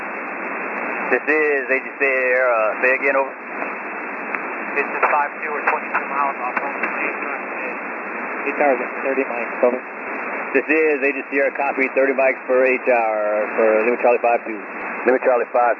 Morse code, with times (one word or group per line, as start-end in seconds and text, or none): none